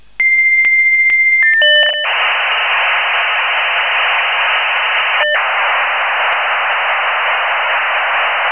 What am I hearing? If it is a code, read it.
Dial-up modem handshake.
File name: modem / v32bis1